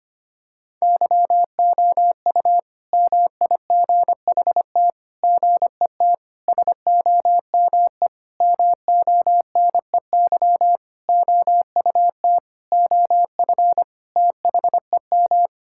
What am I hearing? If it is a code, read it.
Morse code, heard as a keyed tone, about 25 wpm, 700 Hz. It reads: YOU MIGHT GET SOME MONEY OUT OF THEM
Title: Sentences from Top 1000 Words - Review Part 5 - 25wpm